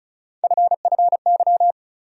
Morse code, heard as a keyed tone, about 34 wpm, 700 Hz. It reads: FFY
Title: USA Call Sign Postfixes - 3 Letters 35wpm